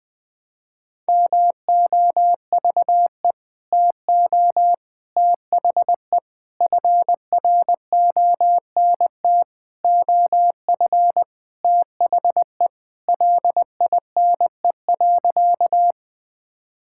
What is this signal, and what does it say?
Morse code, heard as a keyed tone, about 20 wpm, 700 Hz. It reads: MOVE TO THE FRONT OF THE LINE.
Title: Sentences with 5-Word Phrases - 20wpm